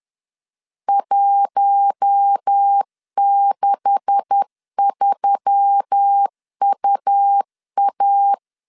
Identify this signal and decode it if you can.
Morse code, heard as a keyed tone, about 10 wpm, 800 Hz. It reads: 163UA
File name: LEZIONI / 9 286 10wpm 2735